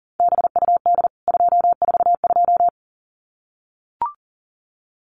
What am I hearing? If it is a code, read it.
Morse code, heard as a keyed tone, about 40 wpm, 700 Hz. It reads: BUD 242
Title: CWT Exchange 40wpm